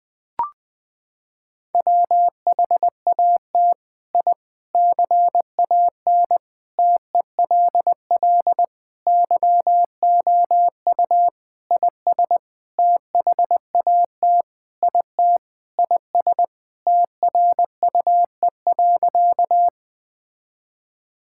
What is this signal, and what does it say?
Morse code, heard as a keyed tone, about 21 wpm, 700 Hz. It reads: WHAT I CAN TELL YOU IS THAT IT IS TRUE.
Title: Sentences with 7-Word Phrases - 20wpm